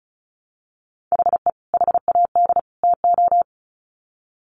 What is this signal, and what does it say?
Morse code, heard as a keyed tone, about 34 wpm, 700 Hz. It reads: HE HAD TO
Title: Sets of 3 Words 35wpm